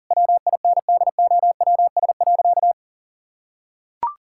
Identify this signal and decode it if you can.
Morse code, heard as a keyed tone, about 40 wpm, 700 Hz. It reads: WINDOWS.
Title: Pangram Challenge v2 - 40wpm